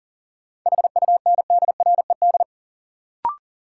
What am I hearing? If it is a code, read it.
Morse code, heard as a keyed tone, about 38 wpm, 700 Hz. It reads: HUNDRED
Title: Sentences from Top 400 Words - New Word Drill - 40wpm